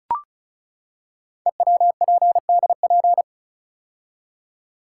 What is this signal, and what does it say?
Morse code, heard as a keyed tone, about 36 wpm, 700 Hz. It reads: EWPDP
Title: Code Groups - 5 Letters - 35wpm